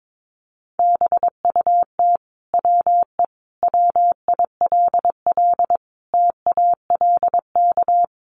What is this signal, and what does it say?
Morse code, heard as a keyed tone, about 22 wpm, 700 Hz. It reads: BUT WE WILL TALK
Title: Sentences from Top 1200 Words 22wpm